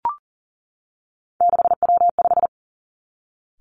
Morse code, heard as a keyed tone, about 41 wpm, 700 Hz. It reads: 6W5
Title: Three Challenging Letter-Number Combinations 40wpm